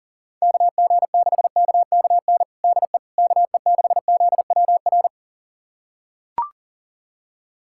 Morse code, heard as a keyed tone, about 40 wpm, 700 Hz. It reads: KG6KKN DE KE6ZWR
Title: USA Callsign DE Callsign - 6 Characters - 40wpm